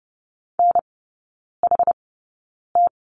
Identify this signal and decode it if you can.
Morse code, heard as a keyed tone, about 30 wpm, spaced as for 9 wpm, 700 Hz. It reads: NHT
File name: 081 Lesson 14e - Number 2 - Three Characters 30wpm